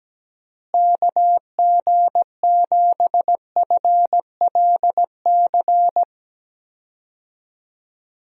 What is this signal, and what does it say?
Morse code, heard as a keyed tone, about 17 wpm, 700 Hz. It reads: KG7FLC
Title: USA Call Signs - 6 Characters - Repeated 2x - 17wpm